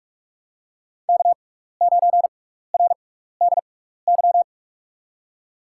Morse code, heard as a keyed tone, about 46 wpm, spaced as for 16 wpm, 700 Hz. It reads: K9RDY
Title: USA Call Signs - 5 Characters - 6x Character Spacing - 45wpm